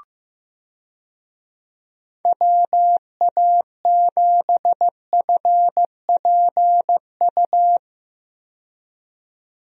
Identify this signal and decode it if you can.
Morse code, heard as a keyed tone, about 15 wpm, 700 Hz. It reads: WA7FPU
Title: USA Call Signs - 6 Characters - 15wpm